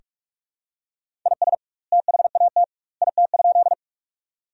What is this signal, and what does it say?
Morse code, heard as a keyed tone, about 46 wpm, 700 Hz. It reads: IS THAT IT?
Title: 2x Word-Spacing - Sentences from Top 100 Words - 45wpm